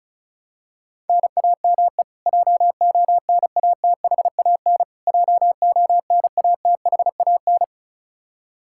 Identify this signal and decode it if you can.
Morse code, heard as a keyed tone, about 34 wpm, 700 Hz. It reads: NAME JONATHAN JONATHAN
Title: Practice QSO - Exchange 1 - Beginning - 35wpm